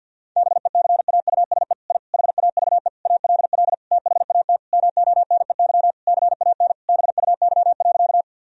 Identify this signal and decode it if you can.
Morse code, heard as a keyed tone, about 50 wpm, 700 Hz. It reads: BECAUSE I HAVE ALL THAT MONEY CAN BUY.
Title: Sentences from Top 1000 Words - Review Part 4 - 50wpm